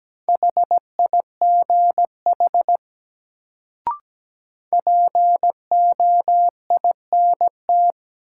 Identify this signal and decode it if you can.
Morse code, heard as a keyed tone, about 17 wpm, 700 Hz. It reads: HIGH POINT
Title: Sentences from Top 200 Words - New Word Drill - 17wpm